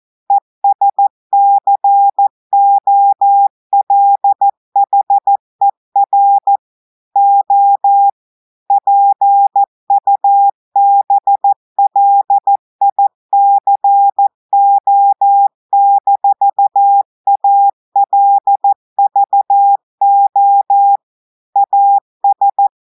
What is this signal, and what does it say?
Morse code, heard as a keyed tone, about 14 wpm, 800 Hz. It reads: ESCOLHER O PUBLICO-ALVO AS
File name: audios / portugais / portugais texte en clair 2 vitesse 14 mots minutes